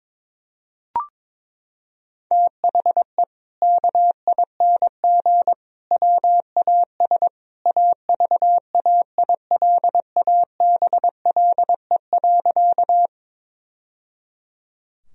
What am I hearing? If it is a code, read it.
Morse code, heard as a keyed tone, about 22 wpm, 700 Hz. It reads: THE KING WAS AVAILABLE.
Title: Sentences from Top 1000 Words - Review Part 3 - 22wpm